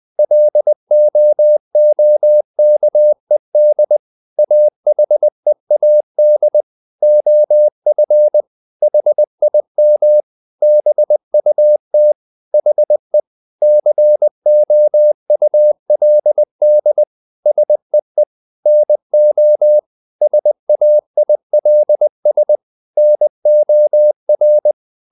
Morse code, heard as a keyed tone, about 20 wpm, 600 Hz. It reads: LOOKED AHEAD OF HIM BUT HE COULD SEE NO SAILS NOR
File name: the old man and the sea part 2 20wpm